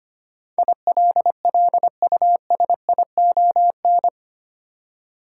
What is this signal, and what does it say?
Morse code, heard as a keyed tone, about 25 wpm, 700 Hz. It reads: ILLUSION
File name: Eight Letter Words - 25wpm